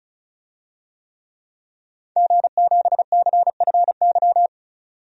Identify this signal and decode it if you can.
Morse code, heard as a keyed tone, about 36 wpm, 700 Hz. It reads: G7CFY
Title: DXCC-100 Call Signs - 5 Characters - 35wpm